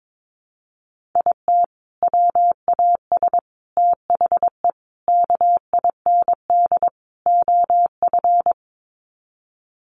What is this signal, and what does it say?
Morse code, heard as a keyed tone, about 22 wpm, 700 Hz. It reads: IT WAS THE KIND OF
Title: Sets of 5 Words Encore2 22wpm